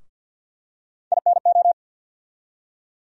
Morse code, heard as a keyed tone, about 49 wpm, 700 Hz. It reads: INO
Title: USA Call Sign Postfixes - 3 Letters 50wpm